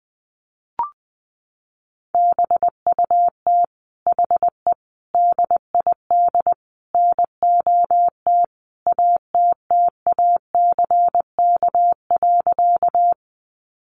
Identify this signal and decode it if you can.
Morse code, heard as a keyed tone, about 20 wpm, 700 Hz. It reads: BUT HE DID NOT ATTACK.
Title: Sentences from Top 1500 Words 20wpm